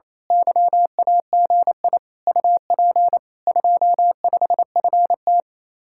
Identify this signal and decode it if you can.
Morse code, heard as a keyed tone, about 27 wpm, 700 Hz. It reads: YAGI UP 25FT
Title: Speed-Racing Common QSO Phrases - 28wpm